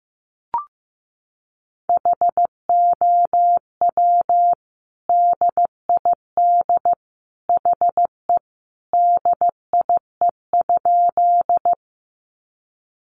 Morse code, heard as a keyed tone, about 15 wpm, 700 Hz. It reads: HOW DID HE DIE?
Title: Sentences from Top 1300 Words 15wpm